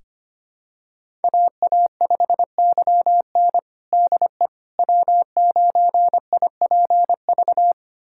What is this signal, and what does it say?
Morse code, heard as a keyed tone, about 25 wpm, 700 Hz. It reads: AA5YN DE W9IPV
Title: USA Callsign DE Callsign - 5 Characters - 25wpm